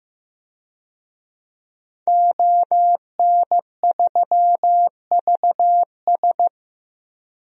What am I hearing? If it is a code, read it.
Morse code, heard as a keyed tone, about 15 wpm, 700 Hz. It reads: ON3VS
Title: DXCC-100 Call Signs - 5 Characters - Repeated 3x - 15wpm